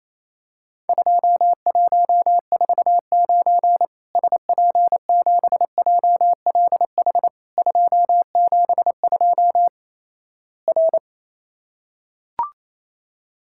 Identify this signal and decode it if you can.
Morse code, heard as a keyed tone, about 28 wpm, 700 Hz. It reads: 2149 SP7JLH 272 R
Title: Worked All Europe DX Contest - QTC Exchange - 28wpm